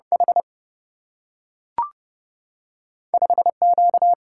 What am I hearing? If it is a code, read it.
Morse code, heard as a keyed tone, about 30 wpm, 700 Hz. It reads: H 5 Q
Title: Single-Letter Single-Number 30wpm